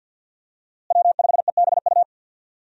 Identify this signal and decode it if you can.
Morse code, heard as a keyed tone, about 47 wpm, 700 Hz. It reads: W5EBU
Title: USA Call Signs - 5 Characters - Repeated 2x - 50wpm